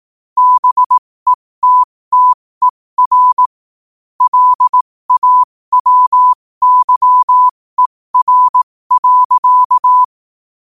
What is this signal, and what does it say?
Morse code, heard as a keyed tone, about 18 wpm, spaced as for 15 wpm, 1000 Hz. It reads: BETTER LAWYER.
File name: qotdfiles / 15wpm